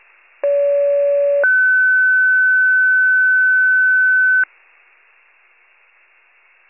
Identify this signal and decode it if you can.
Two-tone page, heard as a one-second tone, then a three-second tone, about 570 then 1500 Hz. It is heard from 0.4 to 4.4 s.